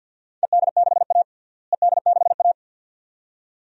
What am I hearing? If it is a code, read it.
Morse code, heard as a keyed tone, about 47 wpm, 700 Hz. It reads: ED6A ED6A
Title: DXCC-100 Call Signs - 4 Characters - Repeated 2x - 50wpm